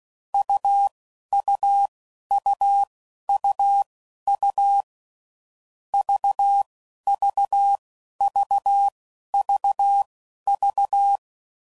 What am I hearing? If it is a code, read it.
Morse code, heard as a keyed tone, about 16 wpm, spaced as for 11 wpm, 800 Hz. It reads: UUUUU VVVVV
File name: Mixed / Short Runs / AUV AUV4